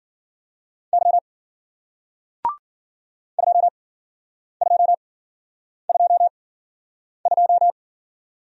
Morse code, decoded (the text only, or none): Y 2222